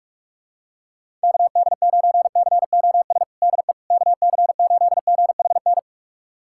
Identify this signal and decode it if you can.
Morse code, heard as a keyed tone, about 45 wpm, 700 Hz. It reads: KD9COS DE KC8GHN
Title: USA Callsign DE Callsign - 6 Characters - 45wpm